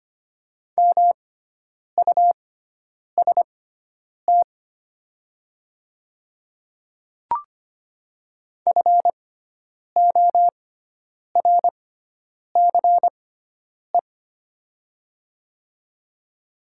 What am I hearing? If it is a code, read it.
Morse code, heard as a keyed tone, about 25 wpm, spaced as for 9 wpm, 700 Hz. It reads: MUST FORCE
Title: 6x Character Spacing - Top 500 Words - 25wpm